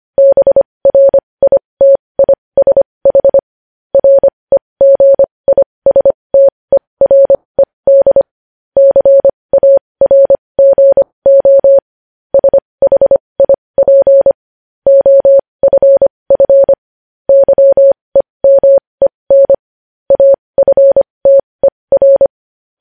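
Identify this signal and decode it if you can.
Morse code, heard as a keyed tone, about 25 wpm, spaced as for 20 wpm, 550 Hz. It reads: BRITISH REGISTERED CARGO SHIP OFF YEMEN AFTER
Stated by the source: Morse code news headlines practice at 20 wpm.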